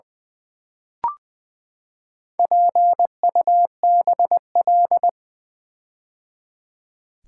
Morse code, heard as a keyed tone, about 20 wpm, 700 Hz. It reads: PUBL
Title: Build A Word - 4 Letters At A Time - 10-Letter Words - 20wpm